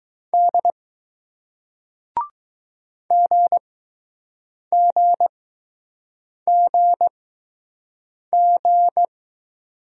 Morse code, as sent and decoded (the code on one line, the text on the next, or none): -.. / --. --. --. --.
D GGGG